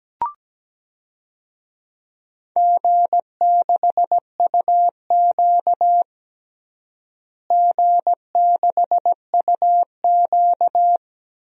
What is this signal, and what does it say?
Morse code, heard as a keyed tone, about 17 wpm, 700 Hz. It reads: G6UQ G6UQ
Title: DXCC-100 Call Signs - 4 Characters - Repeated 3x - 17wpm